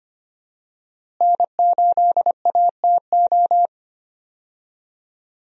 Morse code, heard as a keyed tone, about 25 wpm, 700 Hz. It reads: N8ATO
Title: USA Call Signs - 5 Characters - Repeated 3x - 25wpm